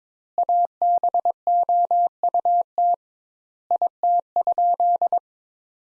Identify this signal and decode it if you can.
Morse code, heard as a keyed tone, about 22 wpm, 700 Hz. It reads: ABOUT IT?
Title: Sentences Top 200 Words - 22wpm